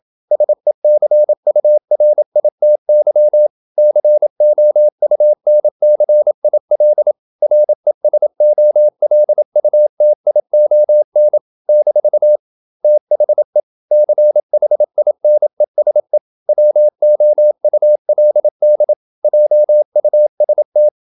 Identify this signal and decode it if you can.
Morse code, heard as a keyed tone, about 27 wpm, 600 Hz. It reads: SECURITY COUNCIL RESOLUTION - THE CHINESE WOULD JUST
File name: Never 0027wpm